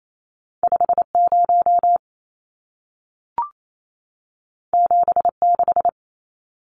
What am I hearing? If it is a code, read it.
Morse code, heard as a keyed tone, about 28 wpm, 700 Hz. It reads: 50 76
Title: Numbers - 2-Digit 28wpm